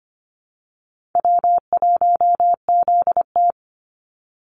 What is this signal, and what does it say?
Morse code, heard as a keyed tone, about 25 wpm, 700 Hz. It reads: W1ZT
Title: Call Signs Repeated 2x - 25wpm